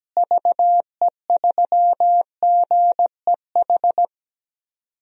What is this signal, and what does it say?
Morse code, heard as a keyed tone, about 17 wpm, 700 Hz. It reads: VE3GEH
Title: DXCC-100 Call Signs - 6 Characters - Repeated 3x - 17wpm